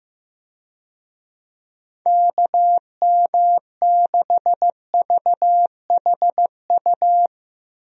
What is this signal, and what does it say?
Morse code, heard as a keyed tone, about 15 wpm, 700 Hz. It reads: KM6VHU
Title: USA Call Signs - 6 Characters - Repeated 3x - 15wpm